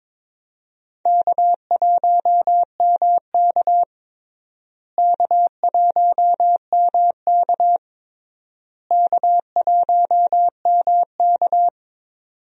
Morse code, heard as a keyed tone, about 22 wpm, 700 Hz. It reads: K1MK K1MK K1MK
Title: USA Call Signs - 4 Characters - Repeated 3x - 22wpm